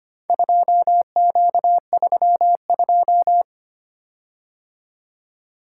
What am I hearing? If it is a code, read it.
Morse code, heard as a keyed tone, about 25 wpm, 700 Hz. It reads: 2Q32
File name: practice / 25wpm / Four Letter-Number Combinations - 25wpm